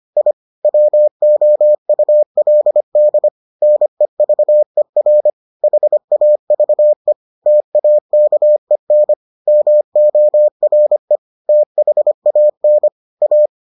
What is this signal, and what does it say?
Morse code, heard as a keyed tone, about 25 wpm, 600 Hz. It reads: I WOULD NEVER HAVE TAKEN MORE THAN A